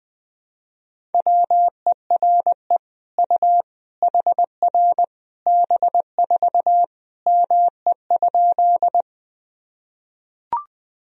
Morse code, heard as a keyed tone, about 20 wpm, 700 Hz. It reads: WERE U HR B4 ME?
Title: Sentences from Top 200 Words and Abbreviations 20wpm